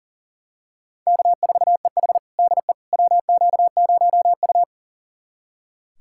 Morse code, heard as a keyed tone, about 40 wpm, 700 Hz. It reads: K4EH DE WQ0U
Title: USA Callsign DE Callsign - 4 Characters - 40wpm